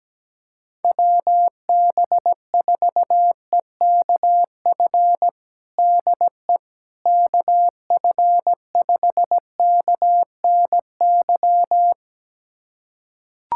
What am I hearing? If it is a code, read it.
Morse code, heard as a keyed tone, about 17 wpm, 700 Hz. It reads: WB4EKF DE KF5KNY
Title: USA Callsign DE Callsign - 6 Characters - 17wpm